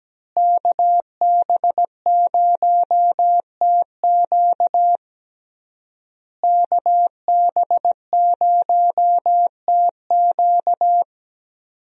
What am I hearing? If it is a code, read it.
Morse code, heard as a keyed tone, about 17 wpm, 700 Hz. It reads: KB0TQ KB0TQ
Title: USA Call Signs - 5 Characters - Repeated 3x - 17wpm